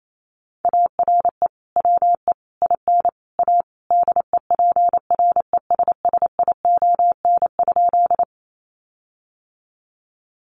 Morse code, heard as a keyed tone, about 28 wpm, 700 Hz. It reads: ARE WE IN A DEPRESSION?
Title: Sentences from Top 1300 Words 28wpm